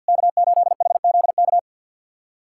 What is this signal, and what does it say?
Morse code, heard as a keyed tone, about 47 wpm, 690 Hz. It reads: K8HZK
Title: SOTA QSO Practice 50wpm